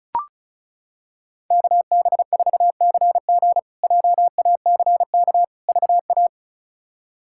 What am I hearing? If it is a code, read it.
Morse code, heard as a keyed tone, about 36 wpm, 700 Hz. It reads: KB4CG JACK VA
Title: K1USN SST QSO Practice - 35wpm